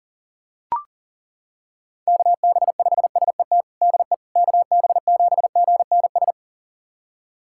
Morse code, heard as a keyed tone, about 40 wpm, 700 Hz. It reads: KB5SET DE KB7GNS
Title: USA Callsign DE Callsign - 6 Characters - 40wpm